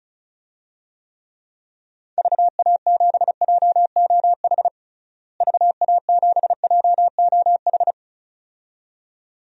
Morse code, heard as a keyed tone, about 34 wpm, 700 Hz. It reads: VA7JOH VA7JOH
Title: DXCC-100 Call Signs - 6 Characters - Repeated 2x - 35wpm